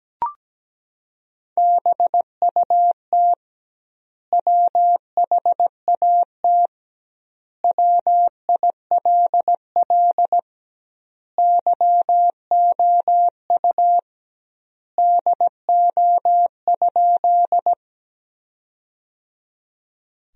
Morse code, heard as a keyed tone, about 17 wpm, 700 Hz. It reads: BUT WHAT WILL YOU DO?
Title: 2x Word-Spacing - Sentences from Top 100 Words - 17wpm